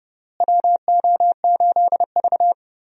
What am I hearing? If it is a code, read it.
Morse code, heard as a keyed tone, about 30 wpm, 700 Hz. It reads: WO8V
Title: USA Call Signs - 4 Characters - Repeated 2x - 30wpm